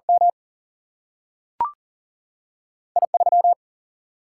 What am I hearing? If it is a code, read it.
Morse code, heard as a keyed tone, about 40 wpm, 700 Hz. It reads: M I 3
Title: Single-Letter Single-Number 40wpm